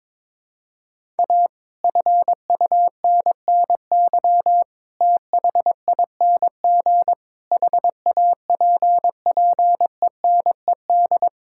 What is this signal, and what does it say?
Morse code, heard as a keyed tone, about 22 wpm, 700 Hz. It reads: A FUNNY THING HAPPENED
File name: Sets of 9 Words - 22wpm